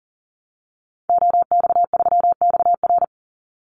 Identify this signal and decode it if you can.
Morse code, heard as a keyed tone, about 40 wpm, 700 Hz. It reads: OX3XR
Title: Call Signs Repeated 3x - 40wpm